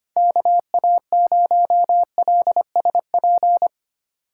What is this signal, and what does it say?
Morse code, heard as a keyed tone, about 25 wpm, 700 Hz. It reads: KA0LSP
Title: USA Call Signs - 6 Characters - 6x Character Spacing - 25wpm